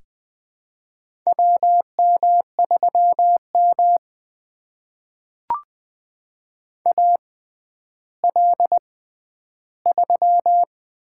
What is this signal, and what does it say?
Morse code, heard as a keyed tone, about 21 wpm, spaced as for 11 wpm, 700 Hz. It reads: WM3M AL3
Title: USA Call Signs - 4 Characters - 6x Character Spacing - 20wpm